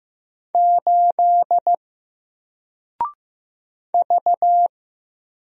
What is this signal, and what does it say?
Morse code, heard as a keyed tone, about 15 wpm, 700 Hz. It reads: 8 V
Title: Single Character (Letter:Number:Punctuation:Prosign) Alt 15wpm